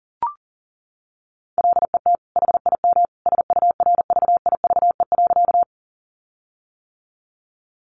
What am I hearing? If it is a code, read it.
Morse code, heard as a keyed tone, about 40 wpm, 700 Hz. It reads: LET HIM SURVIVE.
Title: Sentences from Top 1300 Words 40wpm